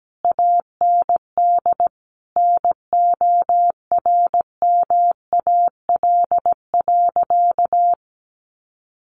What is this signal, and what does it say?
Morse code, heard as a keyed tone, about 17 wpm, 700 Hz. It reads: AND NORMAL.
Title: Sentences from Top 1200 Words 17wpm